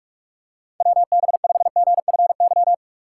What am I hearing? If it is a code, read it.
Morse code, heard as a keyed tone, about 43 wpm, 700 Hz. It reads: WB5GFY
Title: USA Call Signs - 6 Characters - 6x Character Spacing - 45wpm